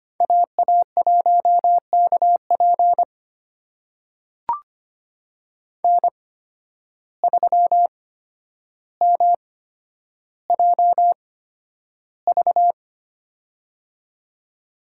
Morse code, heard as a keyed tone, about 25 wpm, spaced as for 11 wpm, 700 Hz. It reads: AA1KP N3MJV
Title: USA Call Signs - 5 Characters - 8x Character Spacing - 25wpm